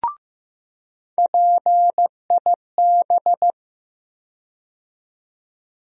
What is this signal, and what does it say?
Morse code, heard as a keyed tone, about 15 wpm, 700 Hz. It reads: PIB
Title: USA Call Sign Postfixes - 3 Letters 15wpm